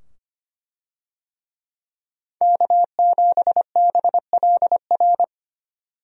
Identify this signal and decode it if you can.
Morse code, heard as a keyed tone, about 25 wpm, 700 Hz. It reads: K7BLR
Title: USA Call Signs - 5 Characters - 25wpm